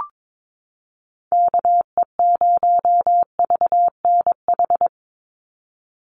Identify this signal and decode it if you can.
Morse code, heard as a keyed tone, about 22 wpm, 700 Hz. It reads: KE0VNH
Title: USA Call Signs 22wpm